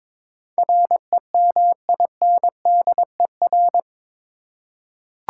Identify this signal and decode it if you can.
Morse code, heard as a keyed tone, about 21 wpm, 700 Hz. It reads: REMINDER
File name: practice / 22wpm / Eight Letter Words - 22wpm